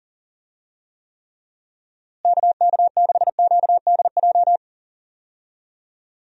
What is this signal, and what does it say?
Morse code, heard as a keyed tone, about 40 wpm, 700 Hz. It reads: KK6QDJ
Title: USA Call Signs - 6 Characters - Repeated 3x - 40wpm